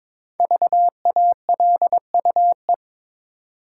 Morse code, heard as a keyed tone, about 22 wpm, 700 Hz. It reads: VALUE
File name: 5 Letter Words - First 3 Letters Given - 22wpm